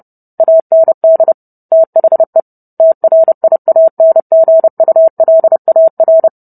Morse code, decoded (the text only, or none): AND THE TRIANGULAR